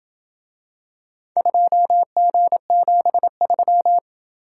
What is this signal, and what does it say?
Morse code, heard as a keyed tone, about 26 wpm, 700 Hz. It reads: 2G73